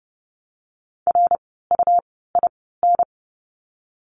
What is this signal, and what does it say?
Morse code, heard as a keyed tone, about 30 wpm, spaced as for 17 wpm, 700 Hz. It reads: RUIN
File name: Lesson 17f - Letter u - Words 30wpm